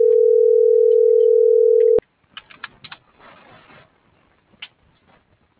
Ringback tone (North America), 440 + 480 Hz.